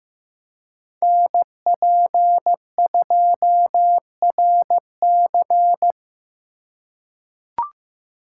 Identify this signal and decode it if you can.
Morse code, heard as a keyed tone, about 15 wpm, 700 Hz. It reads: NP2RC
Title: USA Call Signs - 5 Characters - 3x Character Spacing - 15wpm